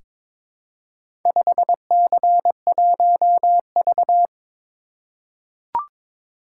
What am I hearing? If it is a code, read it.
Morse code, heard as a keyed tone, about 23 wpm, 700 Hz. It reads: 5C1V
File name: practice / 22wpm / Four Letter-Number Combinations - 22wpm